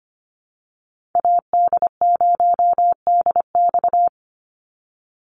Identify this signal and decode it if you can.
Morse code, heard as a keyed tone, about 25 wpm, 700 Hz. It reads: AD0DX
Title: Call Signs - 25wpm